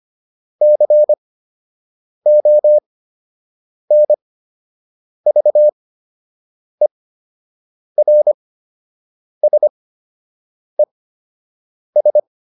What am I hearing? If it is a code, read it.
Morse code, heard as a keyed tone, about 25 wpm, spaced as for 7 wpm, 600 Hz. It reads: CONVERSES